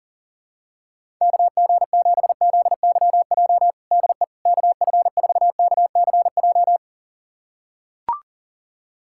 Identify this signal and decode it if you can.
Morse code, heard as a keyed tone, about 40 wpm, 700 Hz. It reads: KG7ZYJ DE KF4KCJ
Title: USA Callsign DE Callsign - 6 Characters - 40wpm